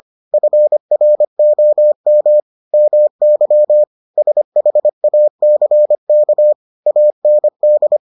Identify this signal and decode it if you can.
Morse code, heard as a keyed tone, about 25 wpm, 600 Hz. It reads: FROM MY SHACK AND